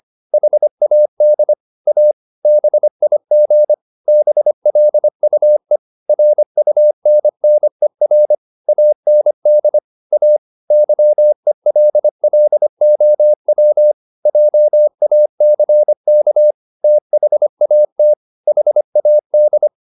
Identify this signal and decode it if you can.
Morse code, heard as a keyed tone, about 25 wpm, 600 Hz. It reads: HAD A BIG BLUE RUNNER AND A YELLOW JACK THAT HAD
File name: the old man and the sea part 1 25wpm